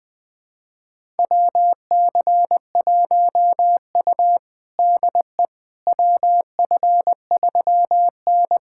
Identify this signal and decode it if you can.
Morse code, heard as a keyed tone, about 21 wpm, 700 Hz. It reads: WC1U DE WF3N
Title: USA Callsign DE Callsign - 4 Characters - 20wpm